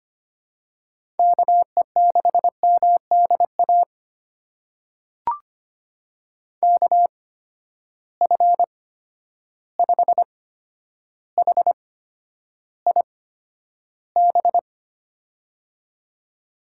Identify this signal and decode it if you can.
Morse code, heard as a keyed tone, about 25 wpm, spaced as for 11 wpm, 700 Hz. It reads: KE6MDA KF5HIB
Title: USA Call Signs - 6 Characters - 8x Character Spacing - 25wpm